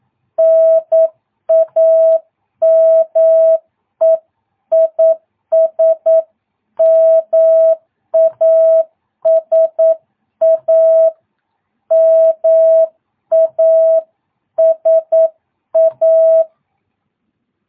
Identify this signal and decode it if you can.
Morse code, heard as a keyed tone, about 9 wpm, 660 Hz. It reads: NAME IS MASA MASA
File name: jh8cbh / 2022cw